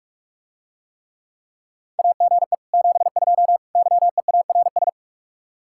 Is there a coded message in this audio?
Yes: Morse code from 2.0 to 4.9 s, heard as a keyed tone, about 44 wpm, 700 Hz. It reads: AGE 72 YEARS